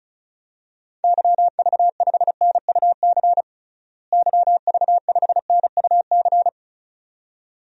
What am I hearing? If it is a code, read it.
Morse code, heard as a keyed tone, about 34 wpm, 700 Hz. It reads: YV5NUC YV5NUC
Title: DXCC-100 Call Signs - 6 Characters - Repeated 2x - 35wpm